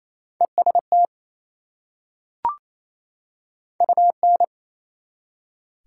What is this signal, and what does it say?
Morse code, heard as a keyed tone, about 27 wpm, 700 Hz. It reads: EST UN
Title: Prefixes and Suffixes 28wpm